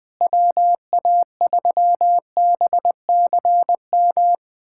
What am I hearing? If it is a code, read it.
Morse code, heard as a keyed tone, about 20 wpm, 700 Hz. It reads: WA3BCM
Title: USA Callsign DE Callsign 20wpm